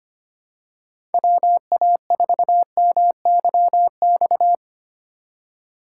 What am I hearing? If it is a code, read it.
Morse code, heard as a keyed tone, about 25 wpm, 700 Hz. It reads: WA4MYX
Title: USA Call Signs - 6 Characters - 3x Character Spacing - 25wpm